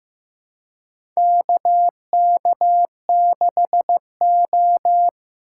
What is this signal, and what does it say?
Morse code, heard as a keyed tone, about 15 wpm, 700 Hz. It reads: KK6O
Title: USA Call Signs - 4 Characters - 4x Character Spacing - 15wpm